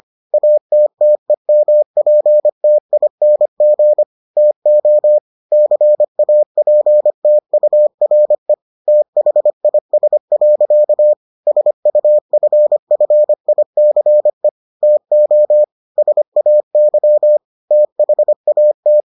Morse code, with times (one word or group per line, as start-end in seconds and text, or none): 0.3-4.0 ATTEMPTING
4.4-5.2 TO
5.5-8.5 CAPTURE
8.9-11.1 THIS.
11.5-14.5 SUFFICE
14.8-15.6 TO
16.0-17.4 SAY
17.7-19.0 THAT